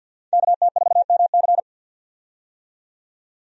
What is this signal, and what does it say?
Morse code, heard as a keyed tone, about 53 wpm, 700 Hz. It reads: KT4MC
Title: USA Call Signs - 5 Characters - 50wpm